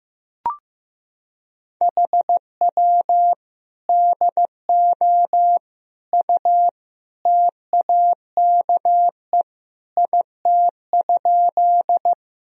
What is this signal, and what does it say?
Morse code, heard as a keyed tone, about 15 wpm, 700 Hz. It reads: HW DO U TAKE IT?
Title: Sentences from Top 200 Words and Abbreviations 15wpm